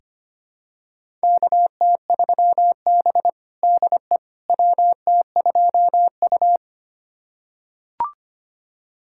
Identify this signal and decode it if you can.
Morse code, heard as a keyed tone, about 25 wpm, 700 Hz. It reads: KT3B DE WT2U
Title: USA Callsign DE Callsign - 4 Characters - 25wpm